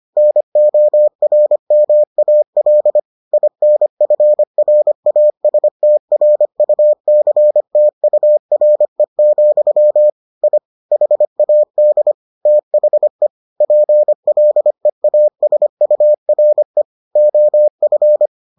Morse code, heard as a keyed tone, about 25 wpm, 600 Hz. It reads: NORMAL INFRASTRUCTURE, I HAD THE PLEASURE OF